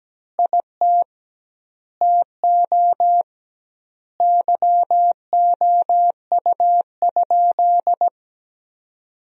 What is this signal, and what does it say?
Morse code, heard as a keyed tone, about 17 wpm, 700 Hz. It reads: IT TO YOU?
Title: 2x Word-Spacing - Sentences from Top 500 Words - 17wpm